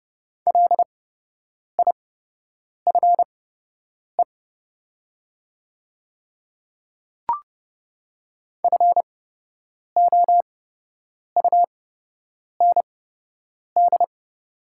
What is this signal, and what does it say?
Morse code, heard as a keyed tone, about 30 wpm, spaced as for 8 wpm, 700 Hz. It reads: LIFE FOUND
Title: 8x Character Spacing - Top 300 Words - 30wpm